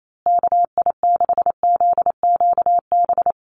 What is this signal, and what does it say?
Morse code, heard as a keyed tone, about 28 wpm, 700 Hz. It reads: KI6ZQB
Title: USA Call Signs Repeated 2x 28wpm